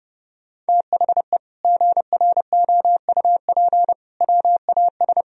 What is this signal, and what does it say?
Morse code, heard as a keyed tone, about 28 wpm, 700 Hz. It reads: THE GROUP WAS
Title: Sentences from Top 1000 Words - Review Part 1 - 30wpm